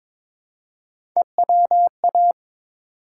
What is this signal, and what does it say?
Morse code, heard as a keyed tone, about 22 wpm, 700 Hz. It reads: EWA
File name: ARRL Sections v2 - 22wpm